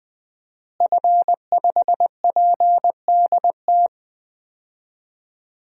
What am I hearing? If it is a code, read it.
Morse code, heard as a keyed tone, about 20 wpm, 700 Hz. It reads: F5PDT
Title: DXCC-100 Call Signs - 5 Characters - Repeated 2x - 20wpm